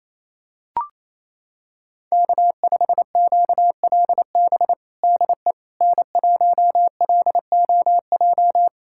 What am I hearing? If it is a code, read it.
Morse code, heard as a keyed tone, about 28 wpm, 700 Hz. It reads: K5QLB DE N1LOJ
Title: USA Callsign DE Callsign - 5 Characters - 28wpm